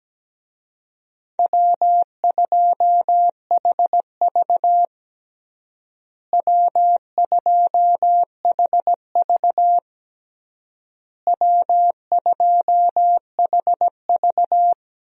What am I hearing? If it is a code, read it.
Morse code, heard as a keyed tone, about 17 wpm, 700 Hz. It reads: W2HV W2HV W2HV
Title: USA Call Signs - 4 Characters - Repeated 3x - 17wpm